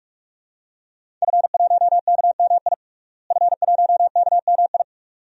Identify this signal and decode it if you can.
Morse code, heard as a keyed tone, about 48 wpm, 700 Hz. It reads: F1KMI F1KMI
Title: DXCC-100 Call Signs - 5 Characters - Repeated 2x - 45wpm